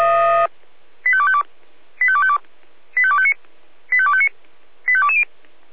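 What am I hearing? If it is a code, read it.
Five-tone selective call (ZVEI), heard as quick runs of five tones, tones of about 70 ms. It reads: 85252 85252 85259 85259 85220
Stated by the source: ZVEI five-tone selective call received on an SDR.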